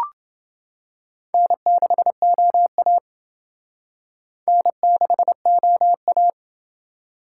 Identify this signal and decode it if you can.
Morse code, heard as a keyed tone, about 28 wpm, 700 Hz. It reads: N6OA N6OA